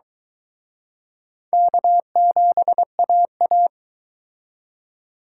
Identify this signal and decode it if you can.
Morse code, heard as a keyed tone, about 23 wpm, 700 Hz. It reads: K7AA